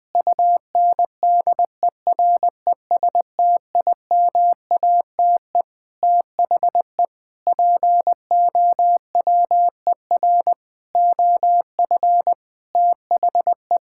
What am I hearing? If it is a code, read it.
Morse code, heard as a keyed tone, about 21 wpm, 700 Hz. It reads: UNDERESTIMATE THE POWER OF THE
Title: Sentences with 4-Word Phrases - Spoken Beginning - 20wpm